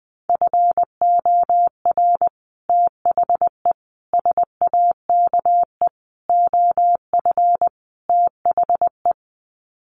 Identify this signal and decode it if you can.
Morse code, heard as a keyed tone, about 20 wpm, 700 Hz. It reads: FOR THE SAKE OF THE
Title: Sets of 5 Words Encore 20wpm